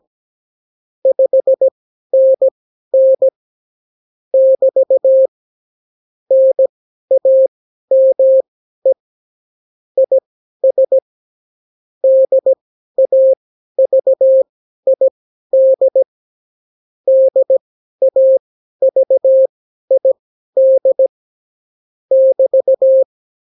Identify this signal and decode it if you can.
Morse code, heard as a keyed tone, about 17 wpm, spaced as for 12 wpm, 530 Hz. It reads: 5NN <BT> NAME IS DAVID DAVID <BT>
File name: CW QSO libroCW4 18-8ppm